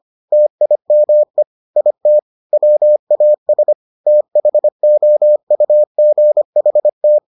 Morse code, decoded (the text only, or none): TIME IT WAS THOUGHT